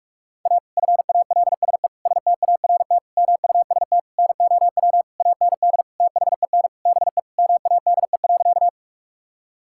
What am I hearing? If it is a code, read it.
Morse code, heard as a keyed tone, about 46 wpm, 700 Hz. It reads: A FALSE START MUST NOW AND THEN BE MADE.
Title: Sentences from Top 1600 Words 45wpm